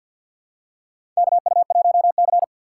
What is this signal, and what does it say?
Morse code, heard as a keyed tone, about 53 wpm, 700 Hz. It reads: KU1C
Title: USA Call Signs - 4x Character Spacing - 50wpm